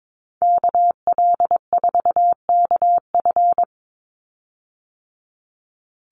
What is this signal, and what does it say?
Morse code, heard as a keyed tone, about 22 wpm, 700 Hz. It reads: KL4KF
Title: USA Call Signs 22wpm